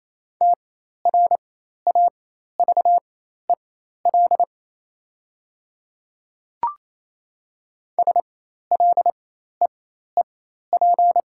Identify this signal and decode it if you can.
Morse code, heard as a keyed tone, about 28 wpm, spaced as for 13 wpm, 700 Hz. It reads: TRAVEL SLEEP
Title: 4x Character Spacing - Top 500 Words - 28wpm